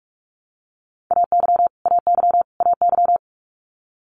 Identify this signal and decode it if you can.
Morse code, heard as a keyed tone, about 46 wpm, 700 Hz. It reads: AY AY AY
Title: Multi-letter Phonemes Repeated 3x 45wpm